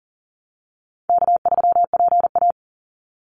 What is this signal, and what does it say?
Morse code, heard as a keyed tone, about 40 wpm, 700 Hz. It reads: K3PA
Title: Call Signs Repeated 2x - 40wpm